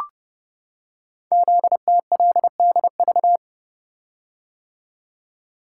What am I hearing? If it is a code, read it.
Morse code, heard as a keyed tone, about 28 wpm, 700 Hz. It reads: ZTLDV
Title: Code Groups - 5 Letters - 30wpm